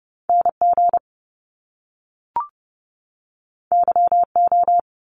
Morse code, heard as a keyed tone, about 30 wpm, 700 Hz. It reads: NG YO
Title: Top Two Letter Combinations in Words 30wpm